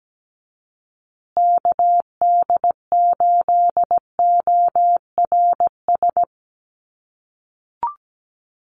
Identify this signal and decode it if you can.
Morse code, heard as a keyed tone, about 17 wpm, 700 Hz. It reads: KD8ORS
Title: USA Call Signs Repeated 2x 17wpm